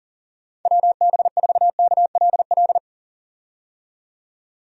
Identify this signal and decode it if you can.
Morse code, heard as a keyed tone, about 40 wpm, 700 Hz. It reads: WB4KLL
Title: USA Call Signs - 6 Characters - 40wpm